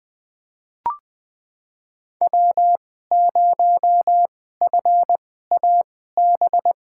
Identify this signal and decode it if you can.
Morse code, heard as a keyed tone, about 20 wpm, spaced as for 14 wpm, 700 Hz. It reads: W0FAB
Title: USA Call Signs - 5 Characters - 2x Character Spacing - 20wpm